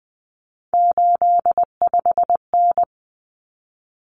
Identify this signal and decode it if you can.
Morse code, heard as a keyed tone, about 20 wpm, 700 Hz. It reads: 85N
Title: Common Cut Numbers - 3-Digit 20wpm